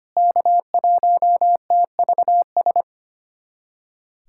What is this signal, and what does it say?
Morse code, heard as a keyed tone, about 25 wpm, 700 Hz. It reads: K1TVS
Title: USA Callsign DE Callsign - 5 Characters - 25wpm